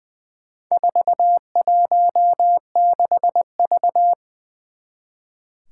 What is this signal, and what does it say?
Morse code, heard as a keyed tone, about 20 wpm, 700 Hz. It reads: 416V
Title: Four Challenging Letter-Number Combinations 20wpm